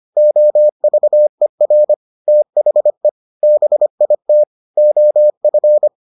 Morse code, heard as a keyed tone, about 25 wpm, 600 Hz. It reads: OVER THE BIT OF